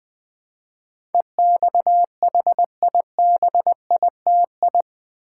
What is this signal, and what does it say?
Morse code, heard as a keyed tone, about 20 wpm, 700 Hz. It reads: EXHIBITI
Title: Build A Word - 4 Letters At A Time - 10-Letter Words - 20wpm